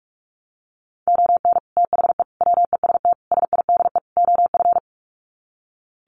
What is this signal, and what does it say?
Morse code, heard as a keyed tone, about 44 wpm, 700 Hz. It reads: ON THE WEST SIDE OF
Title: Sets of 5 Words Encore2 45wpm